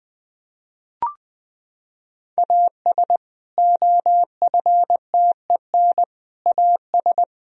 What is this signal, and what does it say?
Morse code, heard as a keyed tone, about 20 wpm, 700 Hz. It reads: AS OFTEN AS
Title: Sentences from Top 1000 Words - Review Part 1 - 20wpm